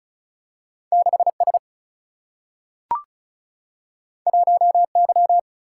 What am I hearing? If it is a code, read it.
Morse code, heard as a keyed tone, about 36 wpm, 700 Hz. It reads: 6 S 1 Y
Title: Single-Letter Single-Number 35wpm